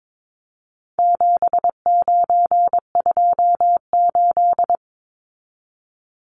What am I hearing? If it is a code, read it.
Morse code, heard as a keyed tone, about 22 wpm, 700 Hz. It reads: 7928
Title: Numbers - 4-Digit 22wpm